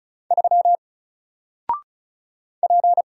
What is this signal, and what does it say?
Morse code, heard as a keyed tone, about 36 wpm, 700 Hz. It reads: 3 P
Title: Single Character (Letter:Number:Punctuation:Prosign) 35wpm